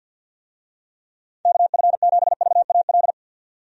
Morse code, heard as a keyed tone, about 47 wpm, 700 Hz. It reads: KF7VAL
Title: USA Call Signs - 6 Characters - 50wpm